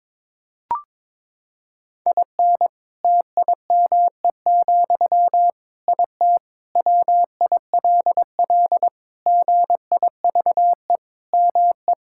Morse code, heard as a keyed tone, about 22 wpm, 700 Hz. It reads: IN TIME, IT WILL GIVE ME
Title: Sentences from Top 1600 Words 22wpm